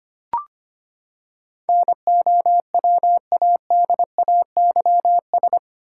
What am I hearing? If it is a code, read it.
Morse code, heard as a keyed tone, about 25 wpm, 700 Hz. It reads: NOWADAYS
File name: Eight Letter Words - 25wpm